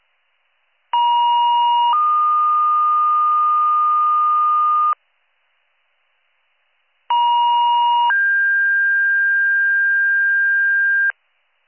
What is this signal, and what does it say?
Two-tone page, heard as a one-second tone, then a three-second tone, about 950 then 1220 Hz, about 950 then 1600 Hz, with noise. It is heard from 0.9 to 11.1 s.